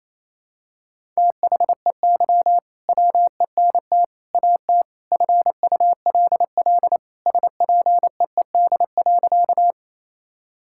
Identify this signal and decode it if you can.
Morse code, heard as a keyed tone, about 27 wpm, 700 Hz. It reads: THEY WENT AT FULL SPEED.
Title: Sentences from Top 1000 Words - Review Part 5 - 28wpm